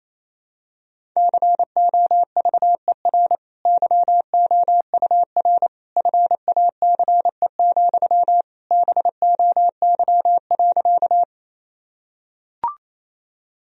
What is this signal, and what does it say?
Morse code, heard as a keyed tone, about 27 wpm, 700 Hz. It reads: COVER YOUR FACE, BOY.
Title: Sentences from Top 1000 Words - Review Part 1 - 28wpm